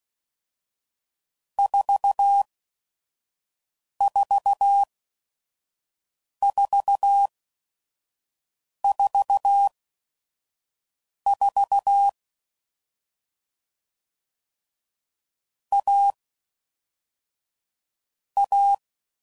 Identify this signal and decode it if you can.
Morse code, heard as a keyed tone, about 16 wpm, spaced as for 5 wpm, 800 Hz. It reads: 44444 A A
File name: Mixed / Short Runs / AUV AUV4